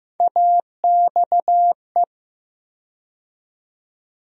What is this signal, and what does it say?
Morse code, heard as a keyed tone, about 15 wpm, 700 Hz. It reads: AXE
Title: USA Call Sign Postfixes - 3 Letters 15wpm